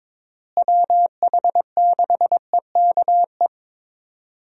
Morse code, heard as a keyed tone, about 22 wpm, 700 Hz. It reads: WH6EKE
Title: USA Call Signs - 6 Characters - Repeated 2x - 22wpm